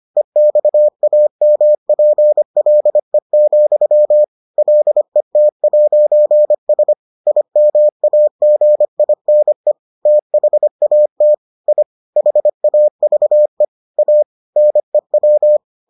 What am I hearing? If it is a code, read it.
Morse code, heard as a keyed tone, about 25 wpm, 600 Hz. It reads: EXAMPLE, LET'S IMAGINE THAT I HAVE A NEW